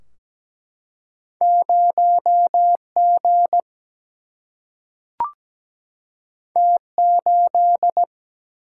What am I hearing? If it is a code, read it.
Morse code, heard as a keyed tone, about 17 wpm, 700 Hz. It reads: 0G T8
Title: Single-Letter Single-Number 17wpm